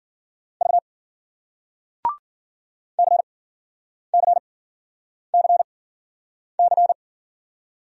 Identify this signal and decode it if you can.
Morse code, heard as a keyed tone, about 55 wpm, spaced as for 9 wpm, 700 Hz. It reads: V CCCC